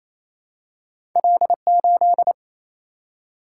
Morse code, heard as a keyed tone, about 28 wpm, 700 Hz. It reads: L8
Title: Single-Letter Single-Number 28wpm